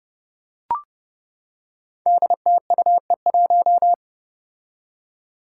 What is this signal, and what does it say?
Morse code, heard as a keyed tone, about 30 wpm, 700 Hz. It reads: DTUE1
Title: Five Challenging Letter-Number Combinations 30wpm